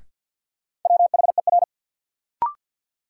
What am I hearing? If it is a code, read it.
Morse code, heard as a keyed tone, about 52 wpm, 700 Hz. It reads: WHER
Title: Sentences from Top 1500 Words - New Word Drill Alt - 50wpm